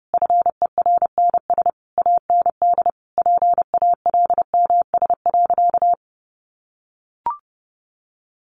Morse code, heard as a keyed tone, about 30 wpm, 700 Hz. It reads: FERNS AND PALMS.
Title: A Sound Of Thunder 30wpm